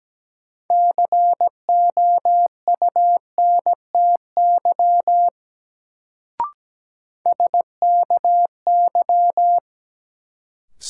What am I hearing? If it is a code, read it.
Morse code, heard as a keyed tone, about 17 wpm, 700 Hz. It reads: COUNTY SKY
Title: Sentences from Top 1200 Words - New Word Drill Alt - 17wpm